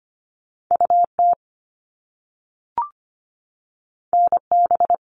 Morse code, heard as a keyed tone, about 25 wpm, 700 Hz. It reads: UT NB
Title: All Cut Numbers - 2-Digit 25wpm